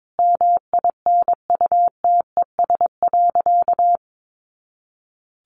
Morse code, heard as a keyed tone, about 22 wpm, 700 Hz. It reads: MINUTES.
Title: A Sound Of Thunder 22wpm